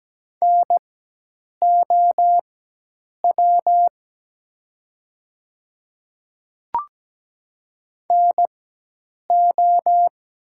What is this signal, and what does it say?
Morse code, heard as a keyed tone, about 17 wpm, spaced as for 8 wpm, 700 Hz. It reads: NOW NO
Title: 4x Character Spacing - Top 100 Words - 17wpm